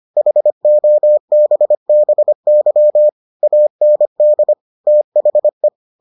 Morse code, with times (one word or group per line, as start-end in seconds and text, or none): 0.2-3.1 HOBBY
3.4-4.5 AND
4.9-5.7 THE